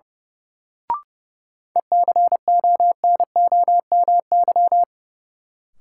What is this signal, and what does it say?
Morse code, heard as a keyed tone, about 30 wpm, 700 Hz. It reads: ECONOMY
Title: Sentences from Top 1200 Words - New Word Drill Alt - 30wpm